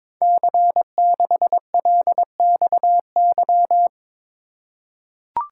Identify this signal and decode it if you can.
Morse code, heard as a keyed tone, about 22 wpm, 700 Hz. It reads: C6LXY
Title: Code Groups - 5 Letter:Numbers - 22wpm